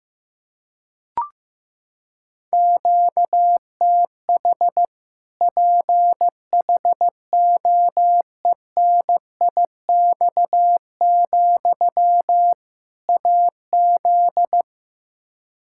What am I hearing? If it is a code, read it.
Morse code, heard as a keyed tone, about 15 wpm, 700 Hz. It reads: QTH PHOENIX, AZ
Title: Practice QSO - Exchange 1 - Beginning - 15wpm